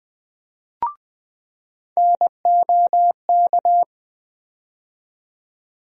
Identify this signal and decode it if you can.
Morse code, heard as a keyed tone, about 20 wpm, 700 Hz. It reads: NOK
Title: USA Call Sign Postfixes - 3 Letters 20wpm